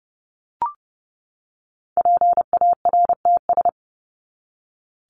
Morse code, heard as a keyed tone, about 30 wpm, 700 Hz. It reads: PARTS
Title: ES:S Suffix - Top 100 Words with Base Words 30wpm